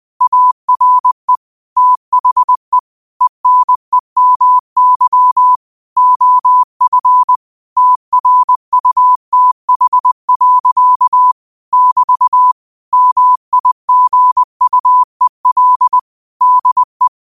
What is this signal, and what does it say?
Morse code, heard as a keyed tone, about 20 wpm, 1000 Hz. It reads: ARE THE ENEMY OF TRUTH. <BT> MIGUEL DE
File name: qotdfiles / 20wpm